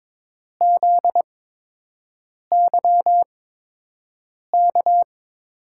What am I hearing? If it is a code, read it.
Morse code, heard as a keyed tone, about 22 wpm, spaced as for 6 wpm, 700 Hz. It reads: ZYK
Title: USA Call Signs - 6 Characters - 8x Character Spacing - 22wpm